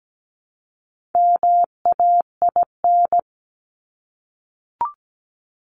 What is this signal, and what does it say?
Morse code, heard as a keyed tone, about 17 wpm, 700 Hz. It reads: MAIN
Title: Top 300 Words 17wpm